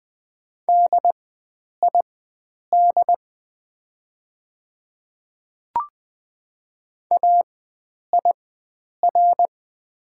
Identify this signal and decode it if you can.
Morse code, heard as a keyed tone, about 20 wpm, spaced as for 9 wpm, 700 Hz. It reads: DID AIR
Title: 4x Character Spacing - Top 300 Words - 20wpm